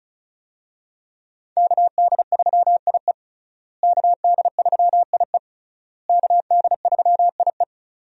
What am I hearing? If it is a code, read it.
Morse code, heard as a keyed tone, about 36 wpm, 700 Hz. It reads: KD3IE KD3IE KD3IE
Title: USA Call Signs - 5 Characters - Repeated 3x - 35wpm